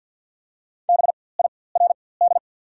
Morse code, heard as a keyed tone, about 54 wpm, spaced as for 24 wpm, 700 Hz. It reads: BIRD
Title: 4x Character Spacing - Top 500 Words - 50wpm